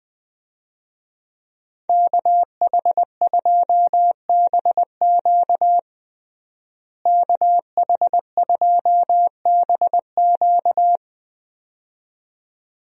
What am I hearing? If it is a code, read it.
Morse code, heard as a keyed tone, about 20 wpm, 700 Hz. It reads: KH2BQ KH2BQ
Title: USA Call Signs - 5 Characters - Repeated 2x - 20wpm